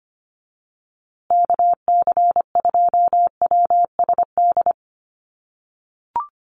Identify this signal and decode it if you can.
Morse code, heard as a keyed tone, about 25 wpm, 700 Hz. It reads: KC2WSD
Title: USA Call Signs Repeated 3x 25wpm